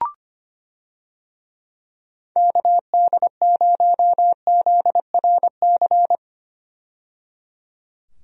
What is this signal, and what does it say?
Morse code, heard as a keyed tone, about 25 wpm, 700 Hz. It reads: KD0ZRC
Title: USA Call Signs - 6 Characters - 25wpm